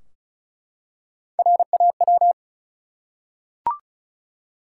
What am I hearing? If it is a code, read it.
Morse code, heard as a keyed tone, about 36 wpm, 700 Hz. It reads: RAW
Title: Three Letter Words - 35wpm